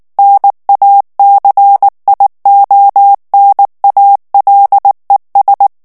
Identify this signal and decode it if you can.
Morse code, heard as a keyed tone, about 19 wpm, 800 Hz. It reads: NACIONALES
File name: morse mp3 / ESPAGNOL TEXTE EN CLAIR 2 VITESSE 18 MOTS MINUTES